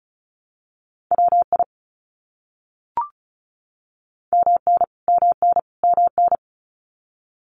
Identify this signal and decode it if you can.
Morse code, heard as a keyed tone, about 34 wpm, 700 Hz. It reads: WI MN MN MN
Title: US States Repeated 3x 35wpm